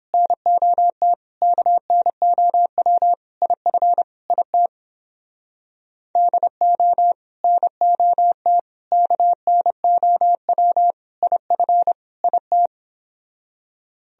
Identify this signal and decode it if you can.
Morse code, heard as a keyed tone, about 29 wpm, 700 Hz. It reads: NOT KNOW IF IT DO NOT KNOW IF IT